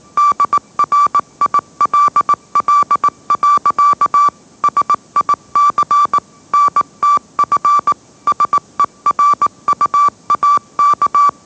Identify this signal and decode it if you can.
Morse code, heard as a keyed tone, about 18 wpm, 1160 Hz. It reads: DRILL. SIC NTF SERUAK
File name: mrx / mrx081 20wpm